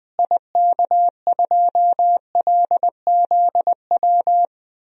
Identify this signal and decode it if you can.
Morse code, heard as a keyed tone, about 20 wpm, 700 Hz. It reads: IK2LZW
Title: DXCC-100 Call Signs - 6 Characters - Repeated 2x - 20wpm